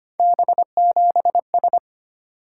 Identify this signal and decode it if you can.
Morse code, heard as a keyed tone, about 25 wpm, 700 Hz. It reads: B7S
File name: Three Letter-Number Combinations - 25wpm